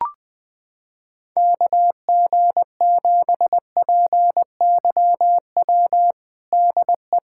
Morse code, heard as a keyed tone, about 20 wpm, 700 Hz. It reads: KG7PYW DE
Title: USA Callsign DE Callsign - 6 Characters - 20wpm